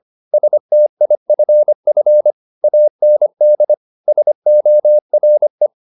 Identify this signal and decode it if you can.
Morse code, heard as a keyed tone, about 25 wpm, 600 Hz. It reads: STIFF AND SORE